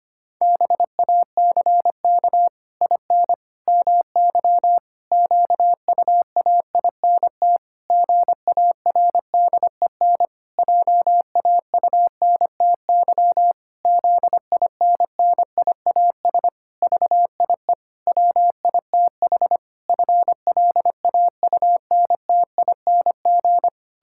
Morse code, heard as a keyed tone, about 25 wpm, 700 Hz. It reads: BACK IN MY QUAINT GARDEN JAUNTY ZINNIAS VIE WITH FLAUNTING
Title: Pangram Challenge v2 - 25wpm